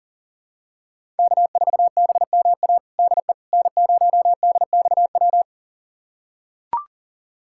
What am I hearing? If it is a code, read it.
Morse code, heard as a keyed tone, about 40 wpm, 700 Hz. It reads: K4BMA DE N0DXW
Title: USA Callsign DE Callsign - 5 Characters - 40wpm